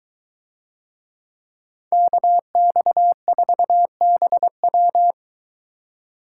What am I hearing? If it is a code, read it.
Morse code, heard as a keyed tone, about 24 wpm, 700 Hz. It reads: KX4BW